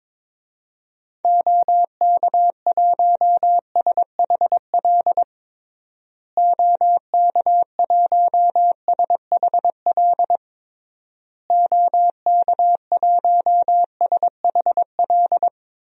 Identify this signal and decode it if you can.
Morse code, heard as a keyed tone, about 22 wpm, 700 Hz. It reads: OK1SHL OK1SHL OK1SHL
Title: DXCC-100 Call Signs - 6 Characters - Repeated 3x - 22wpm